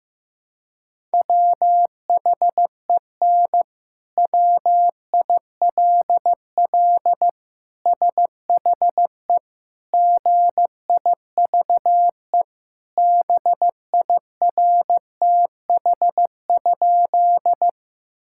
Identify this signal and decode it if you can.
Morse code, heard as a keyed tone, about 15 wpm, 700 Hz. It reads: WHEN WILL SHE GIVE BIRTH?
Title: Sentences from Top 1000 Words - Review Part 5 - 15wpm